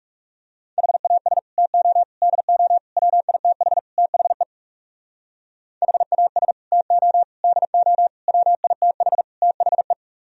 Morse code, heard as a keyed tone, about 39 wpm, 700 Hz. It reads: HAS TO DO WITH THE HAS TO DO WITH THE